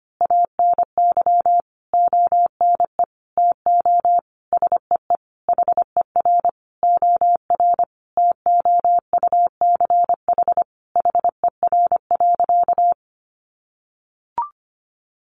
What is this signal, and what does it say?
Morse code, heard as a keyed tone, about 25 wpm, 700 Hz. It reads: ANY ONE TO SEE HER OR TOUCH HER.
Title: Sentences from Top 1200 Words 25wpm